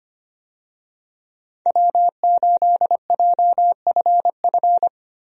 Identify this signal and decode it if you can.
Morse code, heard as a keyed tone, about 25 wpm, 700 Hz. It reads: W8JFF
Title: USA Call Signs - 5 Characters - Repeated 3x - 25wpm